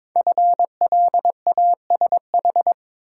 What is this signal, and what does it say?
Morse code, heard as a keyed tone, about 22 wpm, 690 Hz. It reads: FLASH
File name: 5 Letter Words - First 3 Letters Given - 22wpm